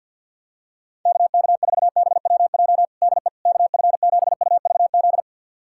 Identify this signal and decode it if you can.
Morse code, heard as a keyed tone, about 46 wpm, 700 Hz. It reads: KK4BWJ DE KF7UVZ
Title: USA Callsign DE Callsign - 6 Characters - 50wpm